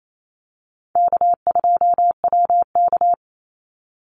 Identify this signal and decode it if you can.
Morse code, heard as a keyed tone, about 28 wpm, 700 Hz. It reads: K2WK
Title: Call Signs - 28wpm